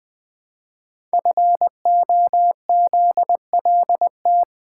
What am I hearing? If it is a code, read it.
Morse code, heard as a keyed tone, about 20 wpm, 700 Hz. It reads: FOZLT
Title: Code Groups - 5 Letters - 20wpm